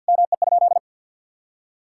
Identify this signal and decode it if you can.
Morse code, heard as a keyed tone, about 50 wpm, 700 Hz. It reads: ME?
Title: Sentences from Top 1600 Words 50wpm